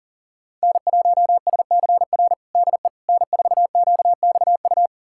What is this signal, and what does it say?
Morse code, heard as a keyed tone, about 40 wpm, 690 Hz. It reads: N1SCR DE N4QXU
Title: USA Callsign DE Callsign - 5 Characters - 40wpm